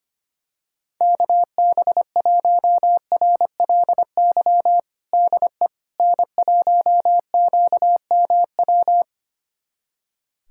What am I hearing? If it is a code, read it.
Morse code, heard as a keyed tone, about 25 wpm, 700 Hz. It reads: KB1RLY DE N1QMW
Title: USA Callsign DE Callsign 25wpm